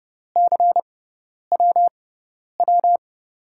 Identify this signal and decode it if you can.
Morse code, heard as a keyed tone, about 30 wpm, spaced as for 10 wpm, 700 Hz. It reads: CWW
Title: USA Call Signs - 6x Character Spacing - 30wpm